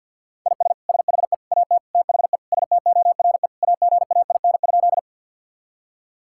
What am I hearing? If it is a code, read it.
Morse code, heard as a keyed tone, about 52 wpm, 700 Hz. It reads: IS SHE AT THE STORE AGAIN?
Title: Sentences from Top 1000 Words - Review Part 5 - 50wpm